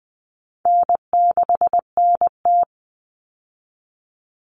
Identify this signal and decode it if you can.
Morse code, heard as a keyed tone, about 20 wpm, 700 Hz. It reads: N6NT
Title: Common Cut Numbers - 4-Digit 20wpm